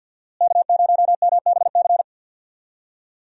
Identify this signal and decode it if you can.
Morse code, heard as a keyed tone, about 52 wpm, 700 Hz. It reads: K0MBC
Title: USA Call Signs - 5 Characters - Repeated 3x - 50wpm